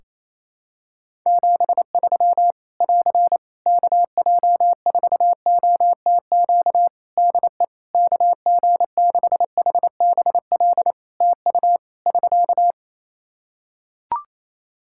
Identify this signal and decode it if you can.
Morse code, heard as a keyed tone, about 28 wpm, 700 Hz. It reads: 73 <AR> KJ4OTQ DE KG6HBL TU <SK>
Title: Practice QSO - Exchange 4 - Ending - 28wpm